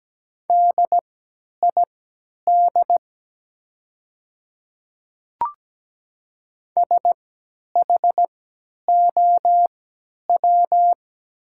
Morse code, heard as a keyed tone, about 17 wpm, spaced as for 10 wpm, 700 Hz. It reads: DID SHOW
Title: 3x Character Spacing - Top 200 Words - 17wpm